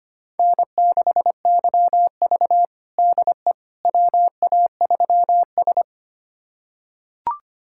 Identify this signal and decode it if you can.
Morse code, heard as a keyed tone, about 25 wpm, 700 Hz. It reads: N6YV DE WA3S
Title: USA Callsign DE Callsign - 4 Characters - 25wpm